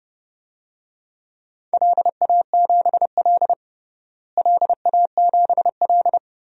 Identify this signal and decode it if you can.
Morse code, heard as a keyed tone, about 32 wpm, 700 Hz. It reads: LA7L LA7L
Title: DXCC-100 Call Signs - 4 Characters - Repeated 3x - 30wpm